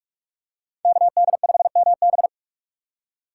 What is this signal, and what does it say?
Morse code, heard as a keyed tone, about 42 wpm, 700 Hz. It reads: KD5MB
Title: USA Call Signs - 5 Characters - 4x Character Spacing - 45wpm